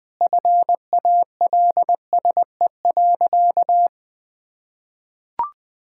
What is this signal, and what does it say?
Morse code, heard as a keyed tone, about 20 wpm, 700 Hz. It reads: FALSE.
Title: Sentences from Top 1600 Words 20wpm